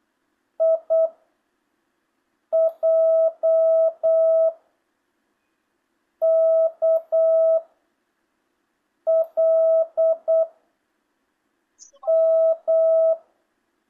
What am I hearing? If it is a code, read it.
Morse code, heard as a keyed tone, about 8 wpm, 650 Hz. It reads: I J K L M
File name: jh8cbh / 2025cw